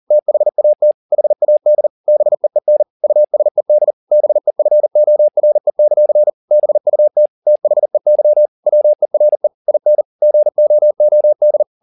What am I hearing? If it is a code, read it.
Morse code, heard as a keyed tone, about 40 wpm, 600 Hz. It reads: THAT HAD BEEN USED BEFORE; BUT THEY WERE IN GOOD